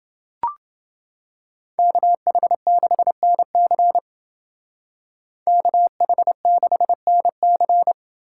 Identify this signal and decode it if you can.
Morse code, heard as a keyed tone, about 28 wpm, 700 Hz. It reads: KH6NC KH6NC